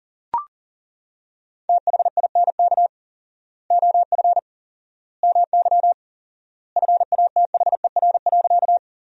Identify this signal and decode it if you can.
Morse code, heard as a keyed tone, about 40 wpm, 700 Hz. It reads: THINK OF MY FATHER.
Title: 4x Word-Spacing - Sentences from Top 300 Words - 40wpm